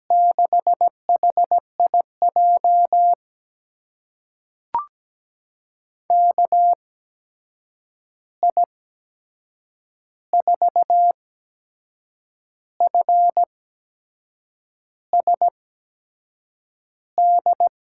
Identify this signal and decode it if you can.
Morse code, heard as a keyed tone, about 17 wpm, spaced as for 6 wpm, 700 Hz. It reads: 6HIJ KI4FSD
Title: USA Call Signs - 8x Character Spacing - 17wpm